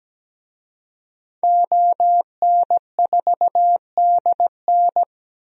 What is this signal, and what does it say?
Morse code, heard as a keyed tone, about 17 wpm, 700 Hz. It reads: ON4DN
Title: DXCC-100 Call Signs - 5 Characters - 17wpm